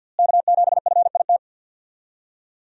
Morse code, heard as a keyed tone, about 51 wpm, 700 Hz. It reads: K7FIT
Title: USA Call Signs - 5 Characters - Repeated 3x - 50wpm